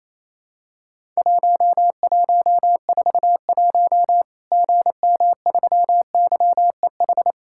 Morse code, heard as a keyed tone, about 28 wpm, 700 Hz. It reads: 1141 GM3YEH
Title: Worked All Europe DX Contest - QTC Exchange - 28wpm